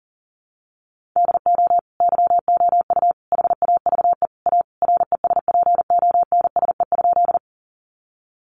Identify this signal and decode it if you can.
Morse code, heard as a keyed tone, about 40 wpm, 700 Hz. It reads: DO YOU HAVE A RESPONSE?
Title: Sentences from Top 1400 Words 40wpm